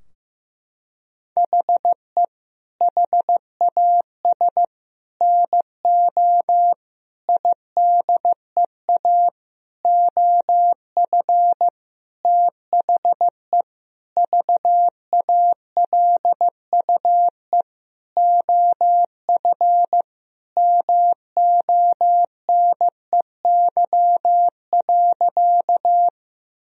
Morse code, heard as a keyed tone, about 15 wpm, 700 Hz. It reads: HE HAS NO IDEA OF THE VALUE OF MONEY.
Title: Sentences from Top 1000 Words - Review Part 3 - 15wpm